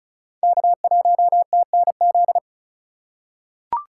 Morse code, heard as a keyed tone, about 36 wpm, 700 Hz. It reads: K1TNZ
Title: USA Call Signs - 5 Characters - 2x Character Spacing - 35wpm